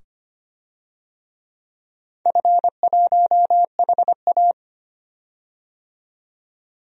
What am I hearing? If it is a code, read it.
Morse code, heard as a keyed tone, about 25 wpm, 700 Hz. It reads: F1HA
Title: DXCC-100 Call Signs - 4 Characters - 25wpm